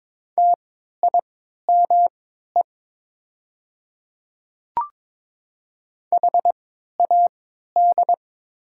Morse code, heard as a keyed tone, about 22 wpm, spaced as for 13 wpm, 690 Hz. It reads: TIME HAD
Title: 3x Character Spacing - Top 100 Words - 22wpm